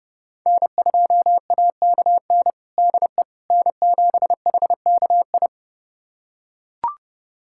Morse code, heard as a keyed tone, about 30 wpm, 700 Hz. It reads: N2AKN DE N7HKI
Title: USA Callsign DE Callsign - 5 Characters - 30wpm